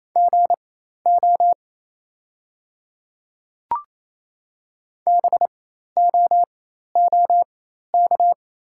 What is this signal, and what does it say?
Morse code, heard as a keyed tone, about 28 wpm, spaced as for 13 wpm, 700 Hz. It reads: GO BOOK
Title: 4x Character Spacing - Top 300 Words - 28wpm